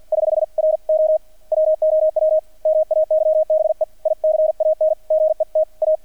Morse code, heard as a keyed tone, about 50 wpm, 630 Hz. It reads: <HH> R O WOW MAYBE ICAN GET A
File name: archived / CRSnet